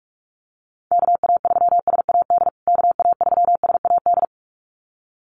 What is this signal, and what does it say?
Morse code, heard as a keyed tone, about 44 wpm, 690 Hz. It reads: KA3SAD KA3SAD
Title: USA Call Signs Repeated 2x 45wpm